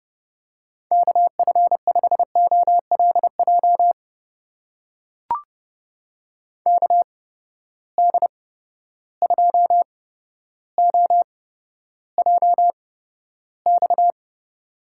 Morse code, heard as a keyed tone, about 32 wpm, spaced as for 13 wpm, 700 Hz. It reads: KF5OLJ KD2OJX
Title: USA Call Signs - 6 Characters - 8x Character Spacing - 30wpm